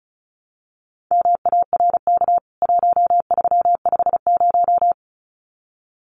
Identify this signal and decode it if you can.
Morse code, heard as a keyed tone, about 34 wpm, 700 Hz. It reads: MARK 1350
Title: CWT Exchange 35wpm